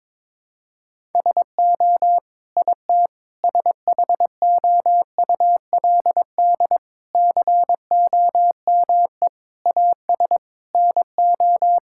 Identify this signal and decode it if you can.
Morse code, heard as a keyed tone, about 22 wpm, 700 Hz. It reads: SO IT SHOULD COME AS NO
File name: Sets of 9 Words - 22wpm